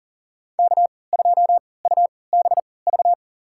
Morse code, heard as a keyed tone, about 40 wpm, spaced as for 23 wpm, 700 Hz. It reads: K2UBV
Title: USA Call Signs - 5 Characters - 3x Character Spacing - 40wpm